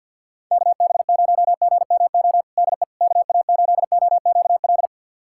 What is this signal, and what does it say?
Morse code, heard as a keyed tone, about 49 wpm, 700 Hz. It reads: KB0GMO DE KA8OQL
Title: USA Callsign DE Callsign 50wpm